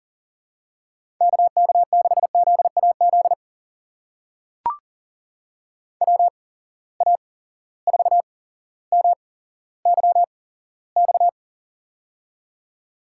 Morse code, heard as a keyed tone, about 39 wpm, spaced as for 17 wpm, 700 Hz. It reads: KK6ZAZ WA4MYX
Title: USA Call Signs - 6 Characters - 8x Character Spacing - 40wpm